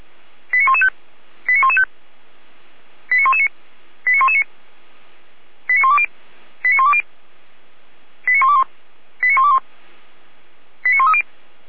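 Five-tone selective call (ZVEI), heard as quick runs of five tones, tones of about 70 ms. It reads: 89115 89115 89119 89119 89120 89120 89121 89121 89130
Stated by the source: Five-tone ZVEI selective call received on SDR, from feuerwehr-orlamuende.org (Feuerwehr Orlamünde, fire brigade).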